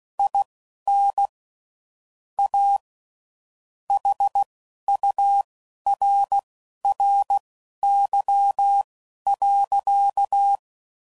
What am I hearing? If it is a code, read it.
Morse code, heard as a keyed tone, about 16 wpm, spaced as for 11 wpm, 800 Hz. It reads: IN A HURRY.
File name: CW Code Groups @10WPM/Technical Subjects